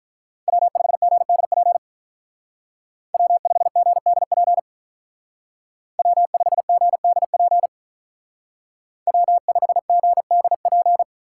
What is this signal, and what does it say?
Morse code, heard as a keyed tone, about 46 wpm, 700 Hz. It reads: W5GDP W5GDP W5GDP W5GDP